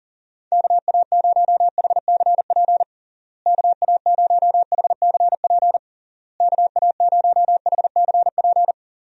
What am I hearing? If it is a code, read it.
Morse code, heard as a keyed tone, about 40 wpm, 700 Hz. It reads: KA0HCP KA0HCP KA0HCP
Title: USA Call Signs - 6 Characters - Repeated 3x - 40wpm